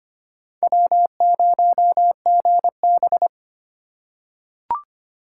Morse code, heard as a keyed tone, about 25 wpm, 700 Hz. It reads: W0GB
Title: USA Call Signs - 4 Characters - 4x Character Spacing - 25wpm